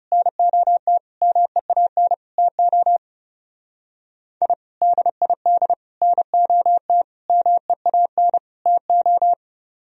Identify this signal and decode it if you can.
Morse code, heard as a keyed tone, about 37 wpm, 700 Hz. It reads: NOT MEAN TO I DID NOT MEAN TO
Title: Speed-Racing - Sets of 5 Words 30wpm